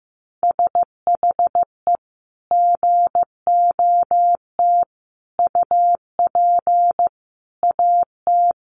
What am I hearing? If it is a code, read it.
Morse code, heard as a keyed tone, about 15 wpm, 700 Hz. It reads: SHE GOT UP AT
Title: Sentences from Top 1100 Words 15wpm